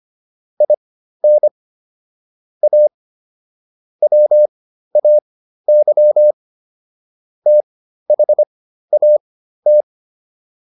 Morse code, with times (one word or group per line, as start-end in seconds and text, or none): 0.6-1.5 IN
2.6-2.9 A
4.0-6.3 WAY
7.5-9.8 THAT